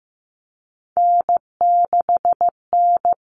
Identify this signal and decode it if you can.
Morse code, heard as a keyed tone, about 15 wpm, 700 Hz. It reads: N6N
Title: All Cut Numbers - 3-Digit 15wpm